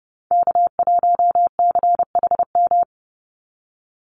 Morse code, heard as a keyed tone, about 30 wpm, 700 Hz. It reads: K1CHM
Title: Call Signs Repeated 3x - 30wpm